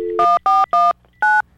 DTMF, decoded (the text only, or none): 1419